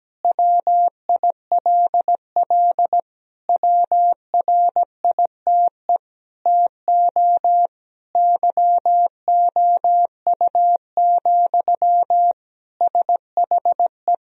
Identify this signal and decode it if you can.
Morse code, heard as a keyed tone, about 17 wpm, 700 Hz. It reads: WILL WRITE TO YOU, SHE
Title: Sentences from Top 1000 Words - Review Part 1 - 17wpm